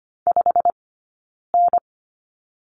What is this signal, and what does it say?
Morse code, heard as a keyed tone, about 25 wpm, 700 Hz. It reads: 5 N
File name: Lesson 15d - Number 5 - Two Characters 25wpm